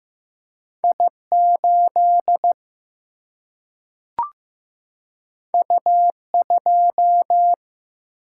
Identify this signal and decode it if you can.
Morse code, heard as a keyed tone, about 15 wpm, 700 Hz. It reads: I8 U2
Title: Single-Letter Single-Number 15wpm